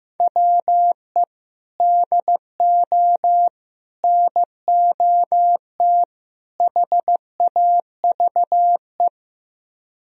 Morse code, heard as a keyed tone, about 16 wpm, 700 Hz. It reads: WE DO NOT HAVE
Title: Speed-Racing - Sets of 4 Words 15wpm